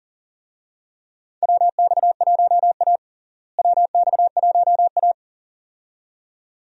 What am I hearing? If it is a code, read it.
Morse code, heard as a keyed tone, about 40 wpm, 700 Hz. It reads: WX1A WX1A
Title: USA Call Signs - 4 Characters - Repeated 2x - 40wpm